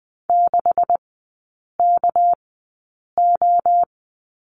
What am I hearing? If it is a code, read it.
Morse code, heard as a keyed tone, about 20 wpm, spaced as for 8 wpm, 700 Hz. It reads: 6KO
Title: 198 Lesson 32e - Letter k - Three Characters 20wpm